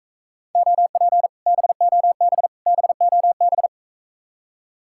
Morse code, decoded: OP BOB BOB

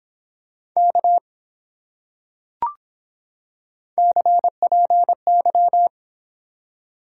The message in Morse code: K CPY

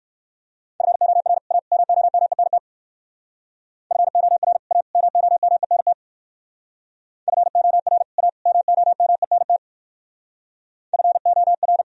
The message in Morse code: FOR A MOMENT FOR A MOMENT FOR A MOMENT FOR

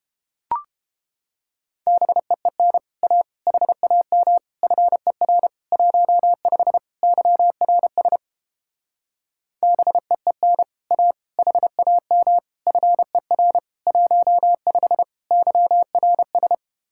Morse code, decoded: BEEN A HAM FER 15 YRS BEEN A HAM FER 15 YRS

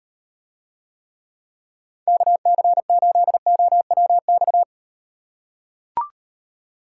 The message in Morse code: KC8OWX

